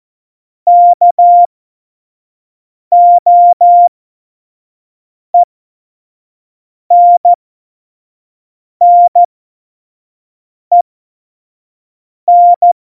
Morse code, heard as a keyed tone, about 14 wpm, spaced as for 5 wpm, 700 Hz. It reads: KOENNEN
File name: Morsekurs / Witze / Witz 05. September 17, Tempo 25